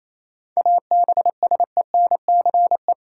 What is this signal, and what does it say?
Morse code, heard as a keyed tone, about 28 wpm, 700 Hz. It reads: ABSENCE
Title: Sets of 4 Words - New Word Drill - 28wpm